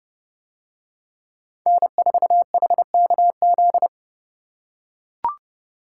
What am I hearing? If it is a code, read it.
Morse code, heard as a keyed tone, about 30 wpm, 700 Hz. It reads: N4HKZ